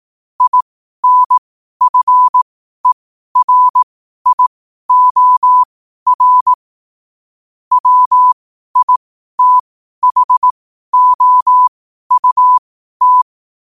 Morse code, heard as a keyed tone, about 18 wpm, spaced as for 12 wpm, 1000 Hz. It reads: INFERIOR WITHOUT